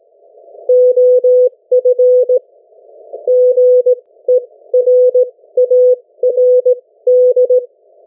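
Morse code, heard as a keyed tone, about 17 wpm, spaced as for 14 wpm, 500 Hz. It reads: OF GERARD
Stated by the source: Morse code from the national amateur radio CW net of New Zealand, which meets on 3535.0 kHz or 7030 kHz.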